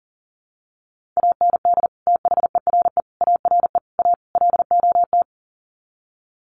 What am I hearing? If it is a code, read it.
Morse code, heard as a keyed tone, about 40 wpm, 700 Hz. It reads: AND THERE ARE A LOT
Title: Sets of 5 Words Encore2 40wpm